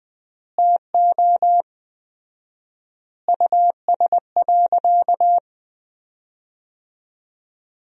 Morse code, heard as a keyed tone, about 20 wpm, 700 Hz. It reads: TO US.
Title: 4x Word-Spacing - Sentences from Top 200 Words - 20wpm